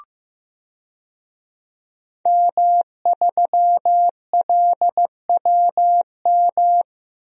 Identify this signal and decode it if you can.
Morse code, heard as a keyed tone, about 15 wpm, 700 Hz. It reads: M3LWM
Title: DXCC-100 Call Signs - 5 Characters - Repeated 2x - 15wpm